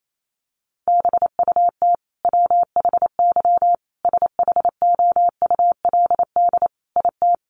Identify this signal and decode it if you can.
Morse code, heard as a keyed tone, about 28 wpm, 700 Hz. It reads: BUT WHY SHOULD IT
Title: Sentences from Top 1200 Words 28wpm